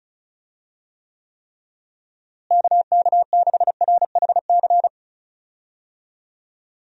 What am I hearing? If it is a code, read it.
Morse code, heard as a keyed tone, about 36 wpm, 700 Hz. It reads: KK6RHC
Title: USA Call Signs - 6 Characters - Repeated 3x - 35wpm